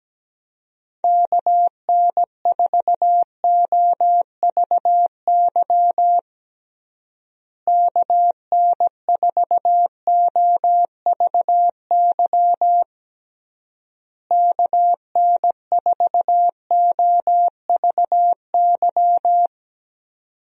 Morse code, heard as a keyed tone, about 17 wpm, 700 Hz. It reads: KN4OVY KN4OVY KN4OVY
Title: USA Call Signs - 6 Characters - Repeated 3x - 17wpm